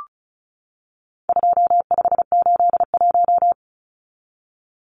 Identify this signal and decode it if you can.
Morse code, heard as a keyed tone, about 34 wpm, 700 Hz. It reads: 2581
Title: Numbers - 4-Digit 35wpm